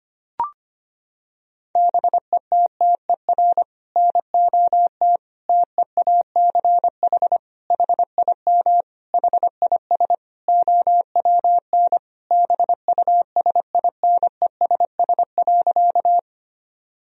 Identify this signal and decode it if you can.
Morse code, heard as a keyed tone, about 25 wpm, 700 Hz. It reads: BETTER NOT TEACH HIM HIS OWN BUSINESS.
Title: Sentences from Top 1000 Words - Review Part 2 - 25wpm